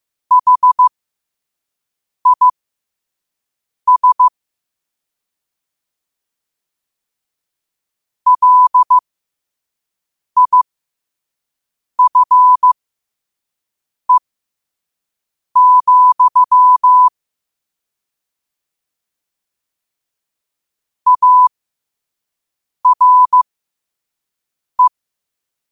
Morse code, heard as a keyed tone, about 15 wpm, spaced as for 5 wpm, 1000 Hz. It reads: HIS LIFE, ARE
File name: qotdfiles / 5wpm